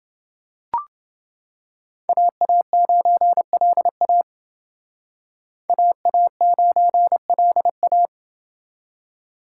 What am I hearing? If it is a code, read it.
Morse code, heard as a keyed tone, about 28 wpm, 700 Hz. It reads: AA9LA AA9LA